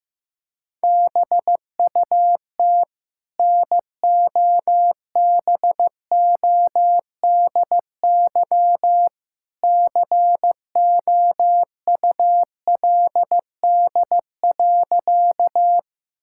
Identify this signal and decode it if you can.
Morse code, heard as a keyed tone, about 15 wpm, 700 Hz. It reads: BUT NOBODY COULD.
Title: Sentences from Top 1600 Words 15wpm